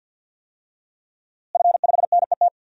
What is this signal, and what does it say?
Morse code, heard as a keyed tone, about 50 wpm, 700 Hz. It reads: F5NET
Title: DXCC-100 Call Signs - 5 Characters - Repeated 3x - 50wpm